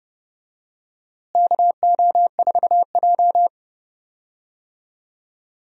Morse code, heard as a keyed tone, about 30 wpm, 700 Hz. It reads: KO4J
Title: USA Call Signs - 4 Characters - Repeated 2x - 30wpm